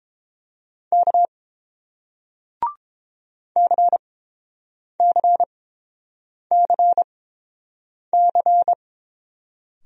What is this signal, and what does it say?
Morse code, heard as a keyed tone, about 29 wpm, spaced as for 7 wpm, 700 Hz. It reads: K CCCC